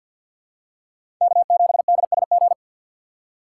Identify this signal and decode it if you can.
Morse code, heard as a keyed tone, about 48 wpm, 700 Hz. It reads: K7DSG
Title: USA Call Signs - 5 Characters - 6x Character Spacing - 50wpm